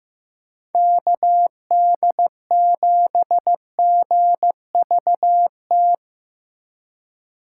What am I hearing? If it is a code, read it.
Morse code, heard as a keyed tone, about 15 wpm, 700 Hz. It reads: KD7GVT
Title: USA Call Signs - 6 Characters - 2x Character Spacing - 15wpm